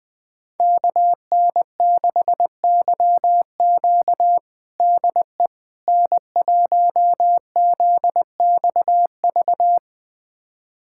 Morse code, heard as a keyed tone, about 20 wpm, 700 Hz. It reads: KN6YQ DE N1ZXV
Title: USA Callsign DE Callsign - 5 Characters - 20wpm